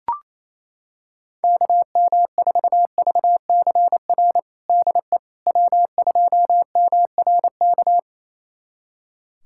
Morse code, heard as a keyed tone, about 28 wpm, 700 Hz. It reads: KM4VCR DE W2MRK
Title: USA Callsign DE Callsign 28wpm